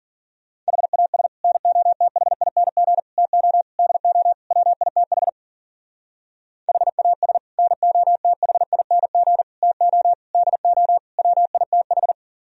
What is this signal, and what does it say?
Morse code, heard as a keyed tone, about 45 wpm, 700 Hz. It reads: HAS NOTHING TO DO WITH HAS NOTHING TO DO WITH
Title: Speed-Racing - Sets of 5 Words 40wpm